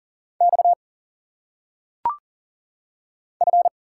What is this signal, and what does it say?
Morse code, heard as a keyed tone, about 40 wpm, 700 Hz. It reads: X F
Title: Single Letters - Phonetic Pronunciation 40wpm